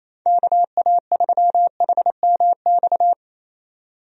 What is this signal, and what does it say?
Morse code, heard as a keyed tone, about 27 wpm, 700 Hz. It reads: KA3HMX
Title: USA Call Signs - 6 Characters - 28wpm